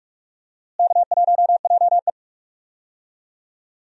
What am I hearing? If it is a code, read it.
Morse code, heard as a keyed tone, about 42 wpm, 700 Hz. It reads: K1JE
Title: USA Call Signs - 4 Characters - Repeated 3x - 45wpm